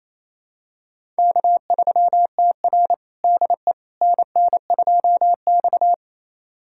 Morse code, heard as a keyed tone, about 28 wpm, 690 Hz. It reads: K3TR DE NN2X
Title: USA Callsign DE Callsign - 4 Characters - 28wpm